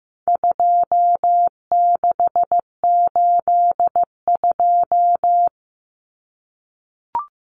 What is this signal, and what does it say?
Morse code, heard as a keyed tone, about 15 wpm, 700 Hz. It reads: 2682
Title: Common Cut Numbers - 4-Digit 15wpm